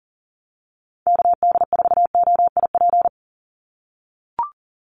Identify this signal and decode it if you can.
Morse code, heard as a keyed tone, about 40 wpm, 700 Hz. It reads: KD4OIP
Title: USA Call Signs 40wpm